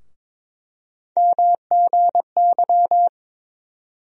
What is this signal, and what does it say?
Morse code, heard as a keyed tone, about 22 wpm, 700 Hz. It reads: MGY
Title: USA Call Sign Postfixes - 3 Letters 22wpm